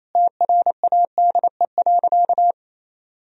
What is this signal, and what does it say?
Morse code, heard as a keyed tone, about 28 wpm, 700 Hz. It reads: TRADE.
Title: Sentences from Top 1000 Words - Review Part 4 - 28wpm